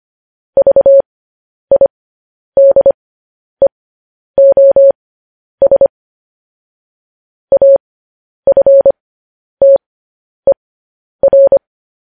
Morse code, heard as a keyed tone, about 25 wpm, spaced as for 10 wpm, 550 Hz. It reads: VIDEOS AFTER
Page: news headlines in morse code at 10 wpm